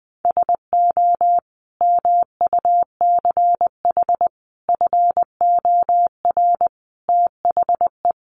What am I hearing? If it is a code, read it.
Morse code, heard as a keyed tone, about 20 wpm, 700 Hz. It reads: SO MUCH FOR THE
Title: Sentences from Top 1000 Words 20wpm